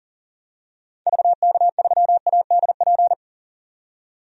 Morse code, heard as a keyed tone, about 37 wpm, 700 Hz. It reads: VK3ADP
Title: DXCC-100 Call Signs - 6 Characters - 40wpm